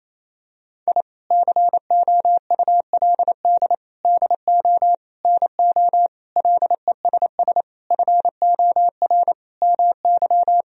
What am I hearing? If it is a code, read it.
Morse code, heard as a keyed tone, about 27 wpm, 700 Hz. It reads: I COULD DO NO LESS FOR MY
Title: Sentences from Top 1000 Words - Review Part 2 - 28wpm